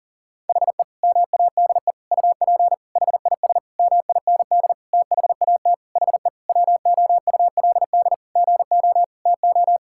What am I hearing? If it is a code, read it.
Morse code, heard as a keyed tone, about 40 wpm, 700 Hz. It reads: HE MADE UP HIS MIND THAT HE WOULD GO TO
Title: Sentences from Top 1600 Words 40wpm